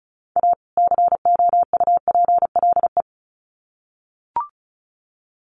Morse code, heard as a keyed tone, about 34 wpm, 700 Hz. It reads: A COUPLE
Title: Sets of 2 Words 35wpm